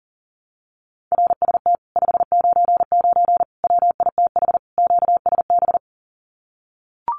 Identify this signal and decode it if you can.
Morse code, heard as a keyed tone, about 40 wpm, 700 Hz. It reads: RST 599 WITH QSB
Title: Common QSO Phrases Encore 40wpm